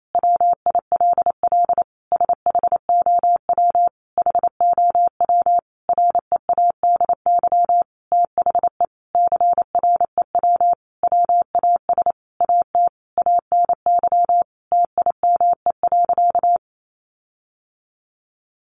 Morse code, heard as a keyed tone, about 28 wpm, 700 Hz. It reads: WILL SHOW HOW READY THE CREW WAS AT ANY TIME.
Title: Sentences from Top 1500 Words 28wpm